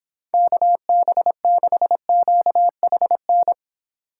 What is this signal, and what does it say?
Morse code, heard as a keyed tone, about 26 wpm, 700 Hz. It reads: KB6QHN